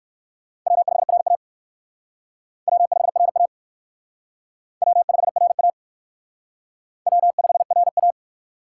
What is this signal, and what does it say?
Morse code, heard as a keyed tone, about 58 wpm, 700 Hz. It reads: W5RA W5RA W5RA W5RA